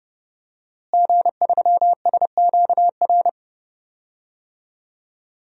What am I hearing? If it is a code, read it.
Morse code, heard as a keyed tone, about 30 wpm, 700 Hz. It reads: G3SQR
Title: DXCC-100 Call Signs - 5 Characters - Repeated 2x - 30wpm